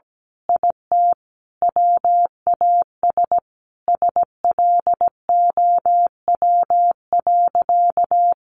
Morse code, heard as a keyed tone, about 17 wpm, 700 Hz. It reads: IT WAS SLOW.
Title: Sentences from Top 1400 Words 17wpm